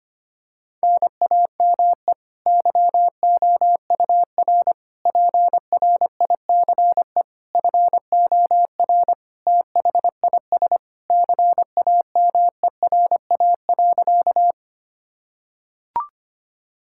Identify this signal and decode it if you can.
Morse code, heard as a keyed tone, about 25 wpm, 700 Hz. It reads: NAME YOUR PRICE FOR THIS CAMERA.
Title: Sentences from Top 1000 Words - Review Part 5 - 25wpm